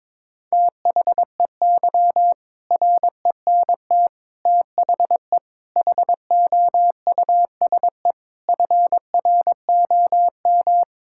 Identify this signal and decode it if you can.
Morse code, heard as a keyed tone, about 22 wpm, 700 Hz. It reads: THEY RENT THE HOUSE FROM
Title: Sentences from Top 1000 Words - Review Part 5 - 22wpm